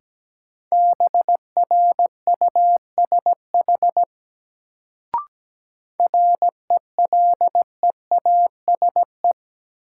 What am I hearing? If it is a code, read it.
Morse code, heard as a keyed tone, about 17 wpm, 700 Hz. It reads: BRUSH RELEASE
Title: Sentences from Top 1100 Words - New Word Drill - 17wpm